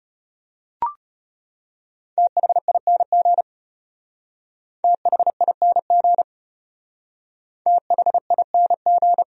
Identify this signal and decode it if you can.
Morse code, heard as a keyed tone, about 34 wpm, 700 Hz. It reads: THING THING THING